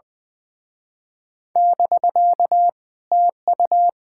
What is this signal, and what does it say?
Morse code, heard as a keyed tone, about 24 wpm, 700 Hz. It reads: <BK> TU
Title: Speed-Racing Common QSO Phrases - 20wpm